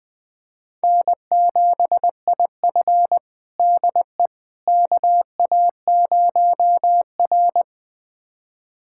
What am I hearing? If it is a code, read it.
Morse code, heard as a keyed tone, about 20 wpm, 700 Hz. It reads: N7IF DE KA0R
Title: USA Callsign DE Callsign - 4 Characters - 20wpm